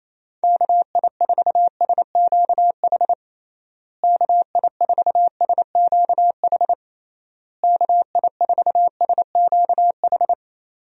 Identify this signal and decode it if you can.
Morse code, heard as a keyed tone, about 28 wpm, 700 Hz. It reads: KI4SQH KI4SQH KI4SQH
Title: USA Call Signs - 6 Characters - Repeated 3x - 28wpm